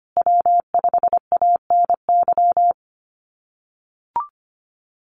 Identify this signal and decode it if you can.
Morse code, heard as a keyed tone, about 25 wpm, 700 Hz. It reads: W5ANY
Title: USA Call Signs Repeated 3x 25wpm